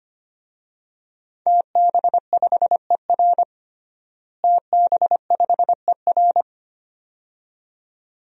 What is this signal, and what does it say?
Morse code, heard as a keyed tone, about 25 wpm, 700 Hz. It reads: TB5ER TB5ER
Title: DXCC-100 Call Signs - 5 Characters - Repeated 2x - 25wpm